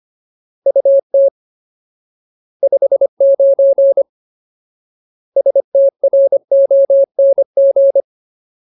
Morse code, heard as a keyed tone, about 25 wpm, 550 Hz. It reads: UT 59 STRONG
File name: Ultimate Word List / 25WPM Ultimate